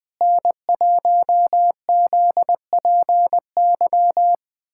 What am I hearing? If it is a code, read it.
Morse code, heard as a keyed tone, about 21 wpm, 700 Hz. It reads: N1ZPY
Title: USA Callsign DE Callsign - 5 Characters - 20wpm